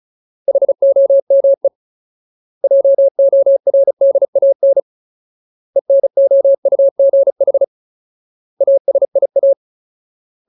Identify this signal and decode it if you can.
Morse code, heard as a keyed tone, about 34 wpm, 550 Hz. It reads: HOME JORDAN ENOUGH ASIA
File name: Ultimate Word List / 35WPM Ultimate